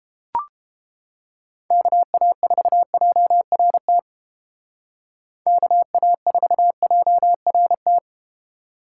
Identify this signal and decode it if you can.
Morse code, heard as a keyed tone, about 31 wpm, 700 Hz. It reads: KA4JRT KA4JRT